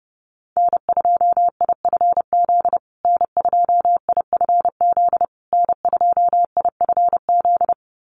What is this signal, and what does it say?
Morse code, heard as a keyed tone, about 30 wpm, 700 Hz. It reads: N2IFZ N2IFZ N2IFZ
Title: USA Call Signs Repeated 3x 30wpm